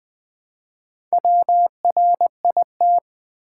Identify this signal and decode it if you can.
Morse code, heard as a keyed tone, about 20 wpm, 700 Hz. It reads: WRIT
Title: Build A Word - 1 Letter At A Time - 5-Letter Words - 20wpm